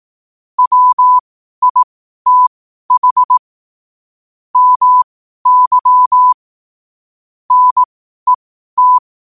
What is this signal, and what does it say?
Morse code, heard as a keyed tone, about 18 wpm, spaced as for 12 wpm, 1000 Hz. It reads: WITH MY NET
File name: 12wpm